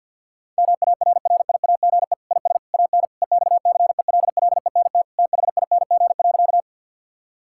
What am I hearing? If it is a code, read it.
Morse code, heard as a keyed tone, about 51 wpm, 700 Hz. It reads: MARRIAGE IS AN EXCELLENT THING.
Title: Sentences from Top 1000 Words - Review Part 5 - 50wpm